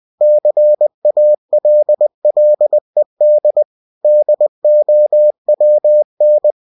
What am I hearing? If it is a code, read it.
Morse code, heard as a keyed tone, about 20 wpm, 600 Hz. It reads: CALLED DOWN